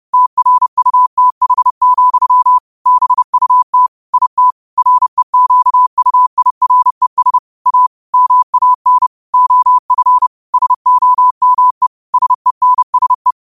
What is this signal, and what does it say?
Morse code, heard as a keyed tone, about 30 wpm, 1000 Hz. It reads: TRUTH, BUT IT REQUIRES A MAN OF SOME SENSE
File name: qotdfiles / 30wpm